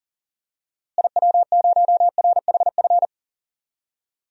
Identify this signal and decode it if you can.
Morse code, heard as a keyed tone, about 40 wpm, 700 Hz. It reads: IW0RHF
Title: DXCC-100 Call Signs - 6 Characters - Repeated 3x - 40wpm